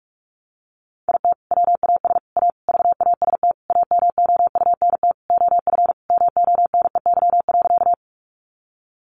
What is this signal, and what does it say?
Morse code, heard as a keyed tone, about 44 wpm, 700 Hz. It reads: IT WAS A VAST AMOUNT OF MONEY.
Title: Sentences from Top 1300 Words 45wpm